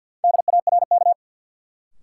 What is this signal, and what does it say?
Morse code, heard as a keyed tone, about 53 wpm, 700 Hz. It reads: DARK
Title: Sentences from Top 500 Words - New Word Drill - 50wpm